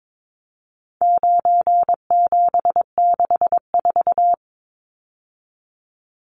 Morse code, heard as a keyed tone, about 22 wpm, 700 Hz. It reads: 9764
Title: Numbers - 4-Digit 22wpm